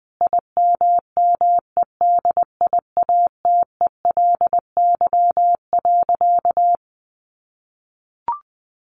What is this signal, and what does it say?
Morse code, heard as a keyed tone, about 20 wpm, 700 Hz. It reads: IMMEDIATELY.
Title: Sentences from Top 1100 Words 20wpm